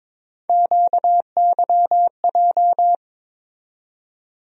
Morse code, heard as a keyed tone, about 22 wpm, 700 Hz. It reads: QYJ
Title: USA Call Sign Postfixes - 3 Letters 22wpm